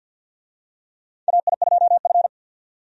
Morse code, heard as a keyed tone, about 48 wpm, 700 Hz. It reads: AI2F
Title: USA Call Signs - 4 Characters - 50wpm